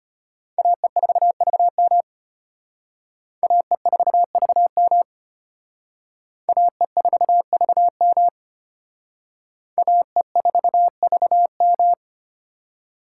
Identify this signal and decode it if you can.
Morse code, heard as a keyed tone, about 31 wpm, 700 Hz. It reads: AE4VM AE4VM AE4VM AE4VM
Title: Speed Racing - USA 2x2 Call Signs 25wpm